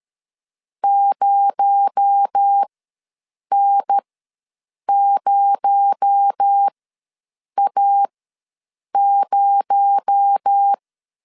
Morse code, heard as a keyed tone, about 13 wpm, spaced as for 7 wpm, 800 Hz. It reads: 0N0A0
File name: LEZIONI / 6 PKX0 / 6 PKX0 ws 12wpm 3448